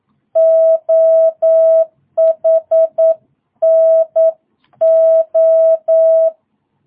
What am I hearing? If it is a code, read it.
Morse code, heard as a keyed tone, about 9 wpm, 660 Hz. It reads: OHNO
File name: jh8cbh / 2022cw